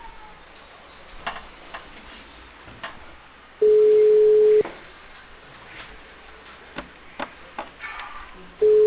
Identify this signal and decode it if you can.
Ringback tone (Europe, 425 Hz).